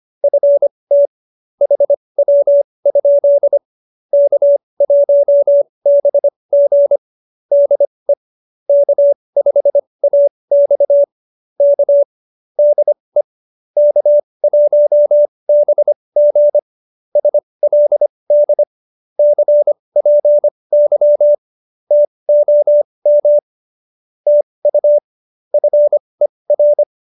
Morse code, heard as a keyed tone, about 25 wpm, spaced as for 20 wpm, 590 Hz. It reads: FT HW? K1BG DE K5AX K DE K1BG SLD CPY TOM TU FER